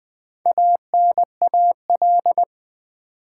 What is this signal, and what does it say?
Morse code, heard as a keyed tone, about 20 wpm, 700 Hz. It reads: ANAL
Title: Build A Word - 2 Letters At A Time - 8-Letter Words - 20wpm